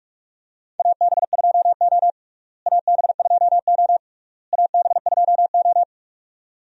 Morse code, heard as a keyed tone, about 47 wpm, 700 Hz. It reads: AB2O AB2O AB2O
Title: USA Call Signs - 4 Characters - Repeated 3x - 45wpm